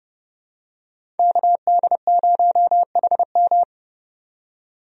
Morse code, heard as a keyed tone, about 31 wpm, 700 Hz. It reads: KD0HM
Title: USA Call Signs - 5 Characters - 30wpm